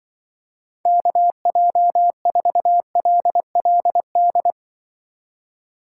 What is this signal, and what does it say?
Morse code, heard as a keyed tone, about 24 wpm, 700 Hz. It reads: KJ4LLD
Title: Speed Racing - USA Call Signs 20wpm